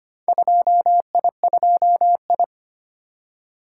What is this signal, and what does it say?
Morse code, heard as a keyed tone, about 25 wpm, 700 Hz. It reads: 2I2I
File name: practice / 25wpm / Four Letter-Number Combinations - 25wpm